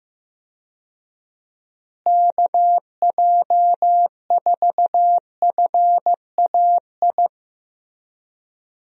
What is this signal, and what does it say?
Morse code, heard as a keyed tone, about 15 wpm, 700 Hz. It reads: KJ4FAI
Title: USA Call Signs - 6 Characters - Repeated 3x - 15wpm